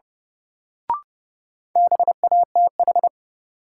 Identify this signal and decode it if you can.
Morse code, heard as a keyed tone, about 30 wpm, 700 Hz. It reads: BATH
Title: Sentences from Top 1600 Words - New Word Drill - 30wpm